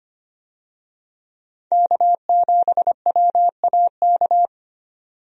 Morse code, heard as a keyed tone, about 25 wpm, 690 Hz. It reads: K7WAK
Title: USA Call Signs - 5 Characters - 25wpm